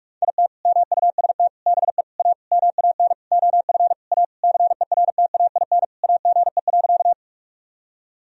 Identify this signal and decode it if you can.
Morse code, heard as a keyed tone, about 41 wpm, 700 Hz. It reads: IT MUST BE A MAN OF A CERTAIN AGE.
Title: Sentences from Top 1000 Words - Review Part 3 - 45wpm